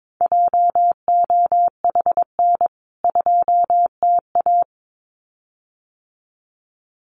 Morse code, heard as a keyed tone, about 22 wpm, 700 Hz. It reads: JOHN 2TA
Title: CWT Exchange 22wpm